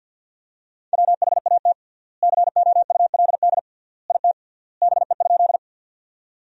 Morse code, heard as a keyed tone, about 50 wpm, 700 Hz. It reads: WHAT COULD IT BE?
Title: 3x Word-Spacing - Sentences from Top 100 Words - 50wpm